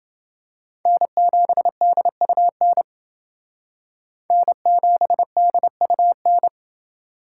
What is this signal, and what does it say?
Morse code, heard as a keyed tone, about 29 wpm, 700 Hz. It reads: N7DUN N7DUN